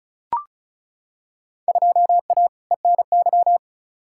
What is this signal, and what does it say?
Morse code, heard as a keyed tone, about 36 wpm, 700 Hz. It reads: 2A ENY
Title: Field Day Exchanges - 35wpm